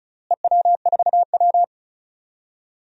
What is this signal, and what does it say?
Morse code, heard as a keyed tone, about 36 wpm, 700 Hz. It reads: EW4W
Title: DXCC-100 Call Signs - 4 Characters - Repeated 3x - 35wpm